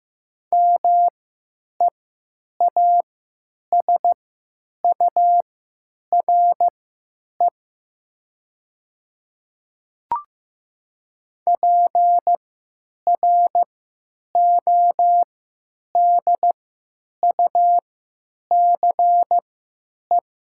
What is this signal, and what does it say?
Morse code, heard as a keyed tone, about 15 wpm, spaced as for 9 wpm, 700 Hz. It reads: MEASURE PRODUCE
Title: 3x Character Spacing - Top 500 Words - 15wpm